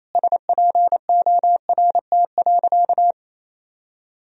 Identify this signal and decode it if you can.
Morse code, heard as a keyed tone, about 28 wpm, 700 Hz. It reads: SPORT.
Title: Sentences from Top 1000 Words - Review Part 4 - 28wpm